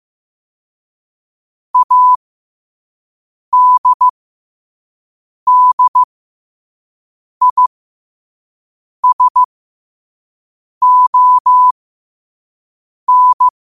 Morse code, heard as a keyed tone, about 15 wpm, spaced as for 5 wpm, 1000 Hz. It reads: ADDISON
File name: qotdfiles / 5wpm